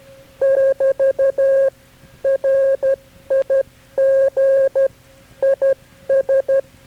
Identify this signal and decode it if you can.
Morse code, heard as a keyed tone, about 12 wpm, 540 Hz. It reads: <BT> RIG IS
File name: KB6MT-tape3-side1